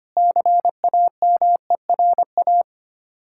Morse code, heard as a keyed tone, about 26 wpm, 690 Hz. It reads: CAMERA
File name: practice / 25wpm / Difficult Words - Speed-Racing with Hints - 25wpm